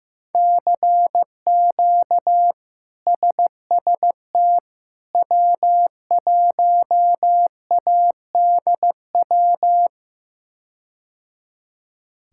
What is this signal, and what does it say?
Morse code, heard as a keyed tone, about 15 wpm, 700 Hz. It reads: CQ SST W1ADW
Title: K1USN SST QSO Practice - 15wpm